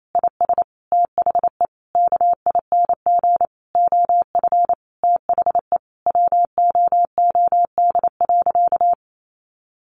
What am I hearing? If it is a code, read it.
Morse code, heard as a keyed tone, about 28 wpm, 700 Hz. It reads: IS THE KING OF THE WOOD.
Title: Sentences from Top 1200 Words 28wpm